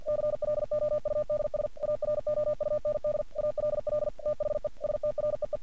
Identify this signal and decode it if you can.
Morse code, heard as a keyed tone, about 51 wpm, 600 Hz. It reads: CLOUDS AROUND ALL THE STREE
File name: archived / CRSnet